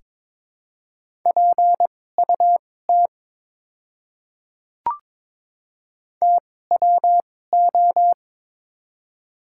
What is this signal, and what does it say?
Morse code, heard as a keyed tone, about 22 wpm, spaced as for 16 wpm, 700 Hz. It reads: PUT TWO
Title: 2x Character Spacing - Top 300 Words - 22wpm